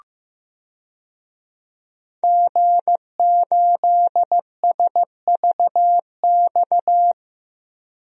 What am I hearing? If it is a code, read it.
Morse code, heard as a keyed tone, about 15 wpm, 700 Hz. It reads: G8SVX
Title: DXCC-100 Call Signs - 5 Characters - 15wpm